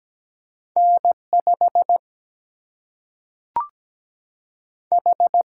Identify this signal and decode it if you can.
Morse code, heard as a keyed tone, about 17 wpm, 700 Hz. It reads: N 5 H
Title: Single-Letter Single-Number 17wpm